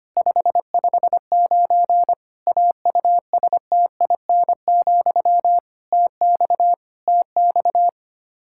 Morse code, heard as a keyed tone, about 25 wpm, 700 Hz. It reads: 559 AUSTIN, TX TX
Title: K1USN SST QSO Practice 25wpm